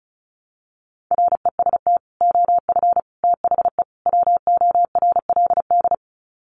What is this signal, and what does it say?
Morse code, heard as a keyed tone, about 34 wpm, 700 Hz. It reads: REST OF THE WORLD
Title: Sets of 4 Words 35wpm